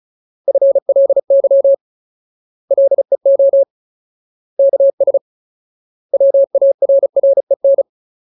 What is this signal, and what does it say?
Morse code, heard as a keyed tone, about 34 wpm, 550 Hz. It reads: FLY LEO KS WARREN
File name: Ultimate Word List / 35WPM Ultimate